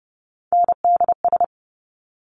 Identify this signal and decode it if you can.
Morse code, heard as a keyed tone, about 30 wpm, 700 Hz. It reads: NDS
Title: Top Three Letter Combinations in Words 30wpm